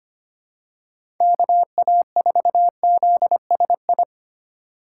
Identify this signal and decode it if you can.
Morse code, heard as a keyed tone, about 25 wpm, 700 Hz. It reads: KA4ZSI
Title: USA Call Signs - 6 Characters - 4x Character Spacing - 25wpm